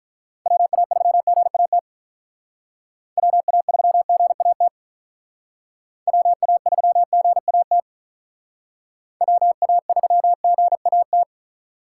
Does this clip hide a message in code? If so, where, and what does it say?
Yes: Morse code from 0.5 to 11.2 s, heard as a keyed tone, about 45 wpm, 700 Hz. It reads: WA3GAT WA3GAT WA3GAT WA3GAT